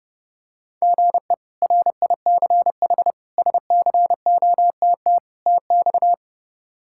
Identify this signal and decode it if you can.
Morse code, heard as a keyed tone, about 32 wpm, 700 Hz. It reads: GE RICH SCOTT TX
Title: K1USN SST QSO Practice - 30wpm